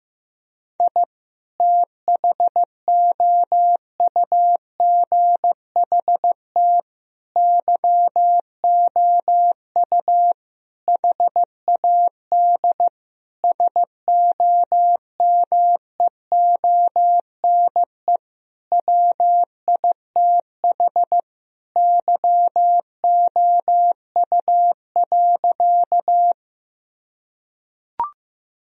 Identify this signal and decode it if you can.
Morse code, heard as a keyed tone, about 15 wpm, 700 Hz. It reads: I THOUGHT YOU HAD SOMEONE WITH YOU.
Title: Sentences from Top 1000 Words - Review Part 4 - 15wpm